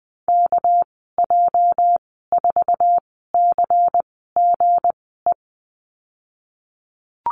Morse code, heard as a keyed tone, about 20 wpm, 700 Hz. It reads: K J 4 C G E
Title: 231 Lesson 37g - Letter x - Call Signs 20wpm